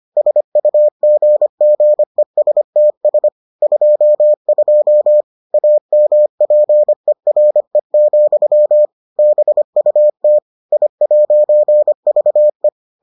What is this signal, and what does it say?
Morse code, heard as a keyed tone, about 25 wpm, 600 Hz. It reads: SUGGESTS 22 AMPERE, BUT I'VE